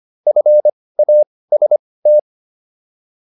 Morse code, heard as a keyed tone, about 25 wpm, spaced as for 18 wpm, 600 Hz. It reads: FAST